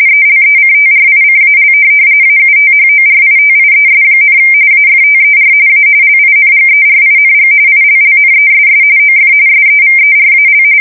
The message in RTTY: ELECTRICAL AUTHORITIES, AFTER A LINGERING ILLNESS OF SEVERAL

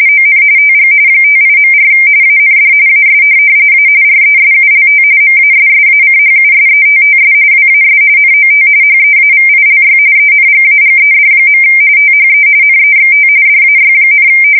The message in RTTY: ON TG 58.4 AND AS A RESULT MASSACHUSETTS WAS NOT HEAVILY ENGAGED. THE NEXT DAY,

